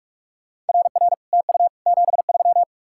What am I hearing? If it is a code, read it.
Morse code, heard as a keyed tone, about 60 wpm, spaced as for 53 wpm, 700 Hz. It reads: RR TU 73
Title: Speed-Racing Common QSO Phrases - 45wpm